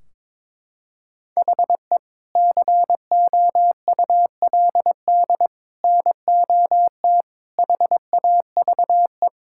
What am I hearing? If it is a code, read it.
Morse code, heard as a keyed tone, about 21 wpm, 700 Hz. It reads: HE COULD NOT HAVE
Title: Sentences from Top 1600 Words 22wpm